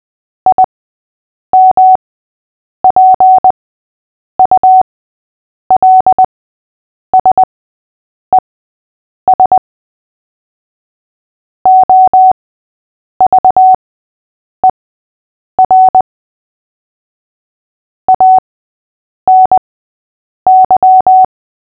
Morse code, heard as a keyed tone, about 20 wpm, spaced as for 8 wpm, 750 Hz. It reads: IMPULSES OVER ANY
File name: cw / fists / Short-Story-6-8wpm